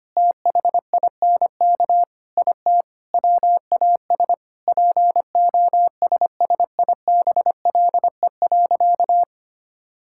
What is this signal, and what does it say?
Morse code, heard as a keyed tone, about 25 wpm, 700 Hz. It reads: THINK IT WAS POSSIBLE.
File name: practice / 25wpm / Sentences with 4-Word Phrases - 25wpm